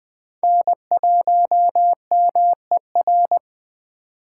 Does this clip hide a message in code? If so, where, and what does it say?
Yes: Morse code from 0.4 to 3.4 s, heard as a keyed tone, about 22 wpm, 700 Hz. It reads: N1MER